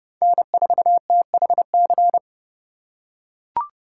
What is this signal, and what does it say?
Morse code, heard as a keyed tone, about 30 wpm, 700 Hz. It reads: N4THC
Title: USA Call Signs - 5 Characters - 4x Character Spacing - 30wpm